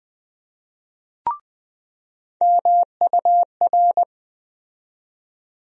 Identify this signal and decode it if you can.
Morse code, heard as a keyed tone, about 20 wpm, 700 Hz. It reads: MUR
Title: USA Call Sign Postfixes - 3 Letters 20wpm